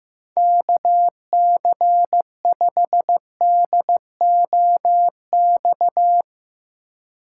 Morse code, heard as a keyed tone, about 15 wpm, 700 Hz. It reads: KC5DOX
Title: USA Call Signs - 6 Characters - 2x Character Spacing - 15wpm